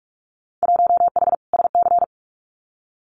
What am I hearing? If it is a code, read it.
Morse code, heard as a keyed tone, about 44 wpm, 700 Hz. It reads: 1H SC
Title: Winter Field Day Exchanges 45wpm